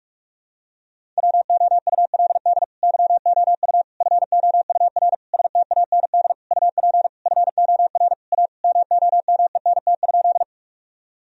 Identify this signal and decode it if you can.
Morse code, heard as a keyed tone, about 43 wpm, 700 Hz. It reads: WOULD YOU FOUR STAND UP FOR A MOMENT?
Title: Sentences from Top 1000 Words - Review Part 1 - 45wpm